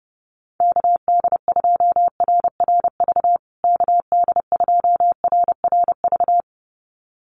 Morse code, heard as a keyed tone, about 30 wpm, 700 Hz. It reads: KD2RRV KD2RRV
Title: USA Call Signs Repeated 2x 30wpm